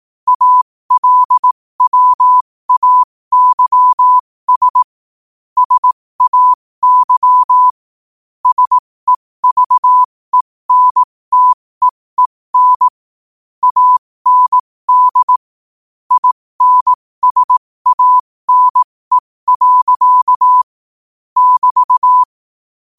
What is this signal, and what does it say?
Morse code, heard as a keyed tone, about 18 wpm, spaced as for 15 wpm, 1000 Hz. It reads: ALWAYS SAY SEVENTEEN AND INSANE. <BT>
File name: qotdfiles / 15wpm